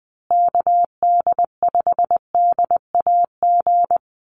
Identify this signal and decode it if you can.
Morse code, heard as a keyed tone, about 20 wpm, 700 Hz. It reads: KD5DAG
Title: USA Call Signs Repeated 2x 20wpm